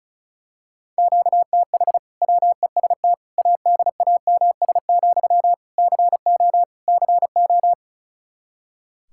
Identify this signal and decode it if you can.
Morse code, heard as a keyed tone, about 34 wpm, 700 Hz. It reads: QTH WEST ADAMS, CO CO
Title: Practice QSO - Exchange 1 - Beginning - 35wpm